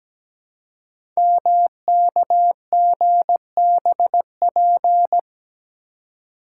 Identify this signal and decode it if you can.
Morse code, heard as a keyed tone, about 17 wpm, 700 Hz. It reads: MKGBP
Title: Code Groups - 5 Letters - 17wpm